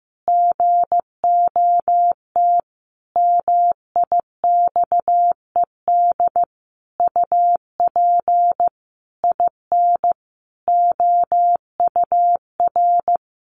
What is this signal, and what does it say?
Morse code, heard as a keyed tone, about 15 wpm, 700 Hz. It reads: GOT MIXED UP IN OUR
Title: Sentences from Top 1500 Words 15wpm